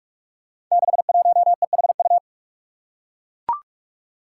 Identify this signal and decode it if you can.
Morse code, heard as a keyed tone, about 41 wpm, 700 Hz. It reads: 61EHU
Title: Five Challenging Letter-Number Combinations 45wpm